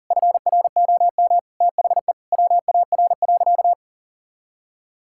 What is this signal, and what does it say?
Morse code, heard as a keyed tone, about 40 wpm, 700 Hz. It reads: FROM THE WAR.
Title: Sentences from Top 1000 Words - Review Part 6 - 40wpm